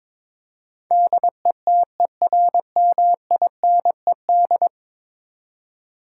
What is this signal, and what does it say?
Morse code, heard as a keyed tone, about 22 wpm, 700 Hz. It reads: DETERMINED
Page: morse-code-ninja-practice-22wpm